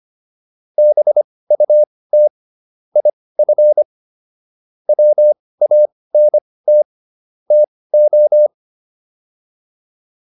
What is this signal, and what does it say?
Morse code, heard as a keyed tone, about 25 wpm, spaced as for 18 wpm, 600 Hz. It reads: BUT IF WANT TO